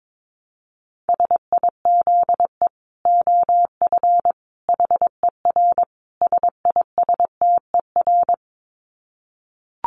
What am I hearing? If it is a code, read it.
Morse code, heard as a keyed tone, about 22 wpm, 700 Hz. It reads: SIZE OF HER SISTER
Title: Dit Challenge 22wpm